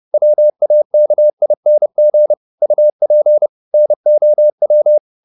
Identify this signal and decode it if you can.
Morse code, heard as a keyed tone, about 30 wpm, 600 Hz. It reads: WAKING UP NOW
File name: the old man and the sea / the old man and the sea 1 30wpm